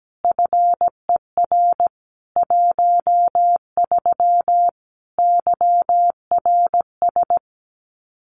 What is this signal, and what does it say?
Morse code, heard as a keyed tone, about 17 wpm, 700 Hz. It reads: FER 13 YRS
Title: Common QSO Phrases 17wpm